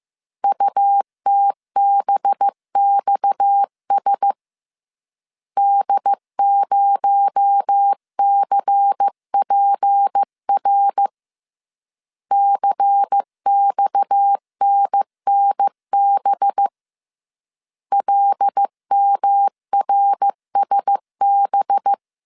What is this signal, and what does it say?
Morse code, heard as a keyed tone, about 15 wpm, 800 Hz. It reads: UTBXS D0CPR CXNNB LMRSB